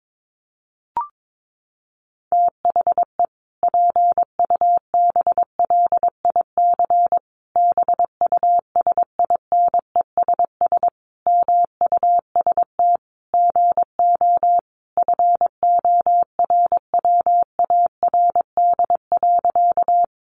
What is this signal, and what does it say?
Morse code, heard as a keyed tone, about 22 wpm, 700 Hz. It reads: THE PUBLIC BUSINESS MUST GO FORWARD.
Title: Sentences from Top 1300 Words 22wpm